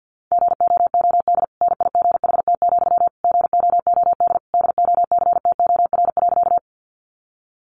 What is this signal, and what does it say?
Morse code, heard as a keyed tone, about 50 wpm, 700 Hz. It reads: GOOD NIGHT, GOOD DOCTOR.
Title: Sentences from Top 1500 Words 50wpm